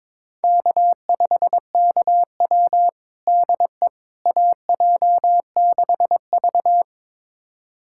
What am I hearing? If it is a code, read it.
Morse code, heard as a keyed tone, about 22 wpm, 700 Hz. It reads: K5KW DE AJ6V
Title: USA Callsign DE Callsign - 4 Characters - 22wpm